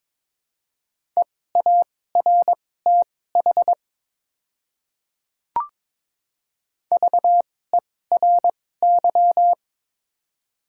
Morse code, heard as a keyed tone, about 23 wpm, spaced as for 16 wpm, 700 Hz. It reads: EARTH VERY
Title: 2x Character Spacing - Top 200 Words - 22wpm